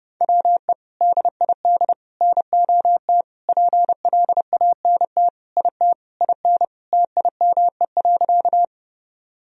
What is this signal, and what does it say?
Morse code, heard as a keyed tone, about 30 wpm, 700 Hz. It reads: WE DID NOT PLANT IT IN TIME.
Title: Sentences from Top 1000 Words - Review Part 1 - 30wpm